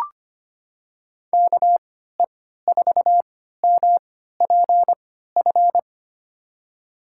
Morse code, heard as a keyed tone, about 25 wpm, spaced as for 14 wpm, 700 Hz. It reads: KE4MPF
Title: USA Call Signs - 6 Characters - 3x Character Spacing - 25wpm